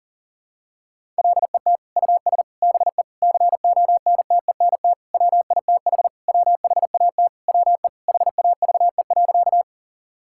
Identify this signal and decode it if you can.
Morse code, heard as a keyed tone, about 40 wpm, 700 Hz. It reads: LET US BE CONTENT WITH WHAT WE HAVE.
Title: Sentences from Top 1000 Words - Review Part 6 - 40wpm